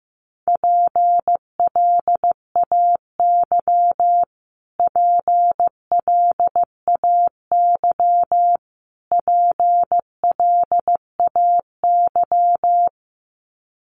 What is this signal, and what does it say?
Morse code, heard as a keyed tone, about 15 wpm, 700 Hz. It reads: PLAY PLAY PLAY
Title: Top 200 Words Repeated 15wpm